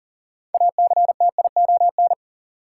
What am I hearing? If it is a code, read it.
Morse code, heard as a keyed tone, about 40 wpm, 700 Hz. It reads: ACTION
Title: Sentences from Top 700 Words - New Word Drill - 40wpm